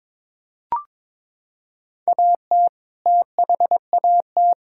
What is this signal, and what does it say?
Morse code, heard as a keyed tone, about 22 wpm, 700 Hz. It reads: AT THAT
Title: Sentences from Top 1600 Words 22wpm